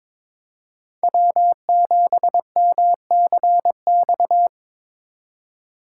Morse code, heard as a keyed tone, about 26 wpm, 700 Hz. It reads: W7MCX
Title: Speed Racing - USA 1x3 Call Signs 22wpm